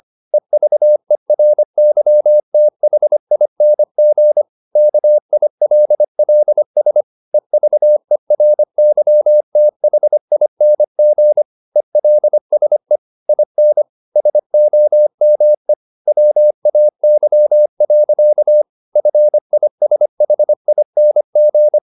Morse code, heard as a keyed tone, about 25 wpm, 600 Hz. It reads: EVERYTHING KILLS EVERYTHING ELSE IN SOME WAY. FISHING